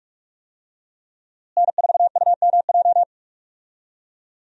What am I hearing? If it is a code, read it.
Morse code, heard as a keyed tone, about 46 wpm, 700 Hz. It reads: N4UMJ
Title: USA Call Signs - 5 Characters - Repeated 2x - 45wpm